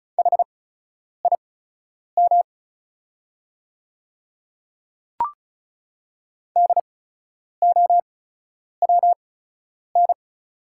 Morse code, heard as a keyed tone, about 35 wpm, spaced as for 10 wpm, 700 Hz. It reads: HIM DOWN
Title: 8x Character Spacing - Top 100 Words - 35wpm